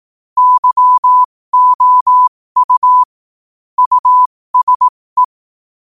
Morse code, heard as a keyed tone, about 18 wpm, spaced as for 15 wpm, 1000 Hz. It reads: YOU USE
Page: 15 WPM morse code quotes